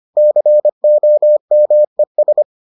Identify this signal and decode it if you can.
Morse code, heard as a keyed tone, about 25 wpm, 600 Hz. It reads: COMES